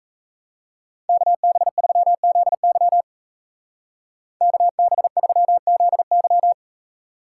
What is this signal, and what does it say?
Morse code, heard as a keyed tone, about 40 wpm, 700 Hz. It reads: KB3ZY KB3ZY